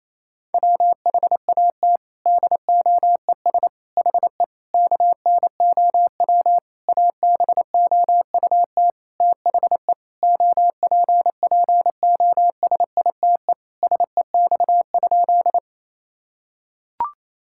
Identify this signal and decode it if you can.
Morse code, heard as a keyed tone, about 28 wpm, 700 Hz. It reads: WHAT DOES HE KNOW ABOUT THE OPPOSITE SEX?
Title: Sentences from Top 1600 Words 28wpm